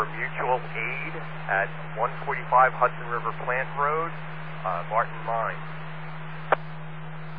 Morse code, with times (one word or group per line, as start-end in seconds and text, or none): none